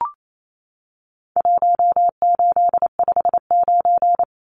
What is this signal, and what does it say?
Morse code, heard as a keyed tone, about 28 wpm, 700 Hz. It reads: 1859
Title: Numbers - 4-Digit 28wpm